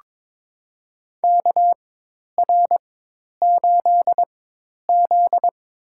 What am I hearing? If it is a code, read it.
Morse code, heard as a keyed tone, about 22 wpm, spaced as for 10 wpm, 700 Hz. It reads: KR8Z
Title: USA Call Signs - 4x Character Spacing - 22wpm